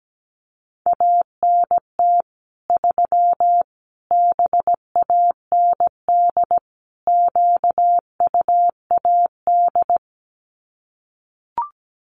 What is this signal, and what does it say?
Morse code, heard as a keyed tone, about 17 wpm, 700 Hz. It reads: ANT 3 BAND QUAD
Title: Common QSO Phrases 17wpm